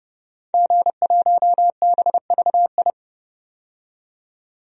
Morse code, heard as a keyed tone, about 32 wpm, 700 Hz. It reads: G1BVI
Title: DXCC-100 Call Signs - 5 Characters - 30wpm